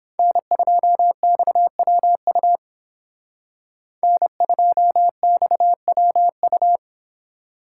Morse code, heard as a keyed tone, about 31 wpm, 700 Hz. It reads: N2XWU N2XWU